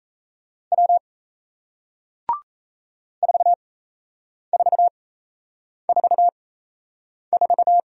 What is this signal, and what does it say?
Morse code, heard as a keyed tone, about 38 wpm, spaced as for 8 wpm, 700 Hz. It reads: W 4444